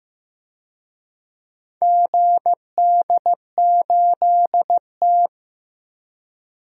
Morse code, heard as a keyed tone, about 15 wpm, 700 Hz. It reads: GD8T
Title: DXCC-100 Call Signs - 4 Characters - 15wpm